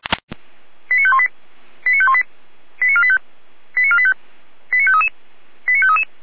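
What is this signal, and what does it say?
Five-tone selective call (ZVEI), heard as quick runs of five tones, tones of about 70 ms. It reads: 89518 89518 89485 89485 89533 89533
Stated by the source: ZVEI five-tone selective call to address 89456, received via SDR.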